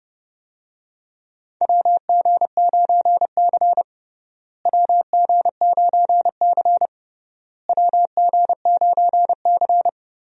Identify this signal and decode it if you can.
Morse code, heard as a keyed tone, about 30 wpm, 700 Hz. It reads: WG9C WG9C WG9C
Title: USA Call Signs - 4 Characters - Repeated 3x - 30wpm